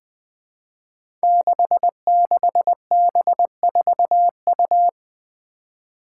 Morse code, heard as a keyed tone, about 20 wpm, 700 Hz. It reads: 66B4U
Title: Five Challenging Letter-Number Combinations 20wpm